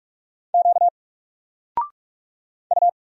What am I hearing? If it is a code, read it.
Morse code, heard as a keyed tone, about 44 wpm, 700 Hz. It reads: Q U
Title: Single Character (Letter:Number:Punctuation:Prosign) 45wpm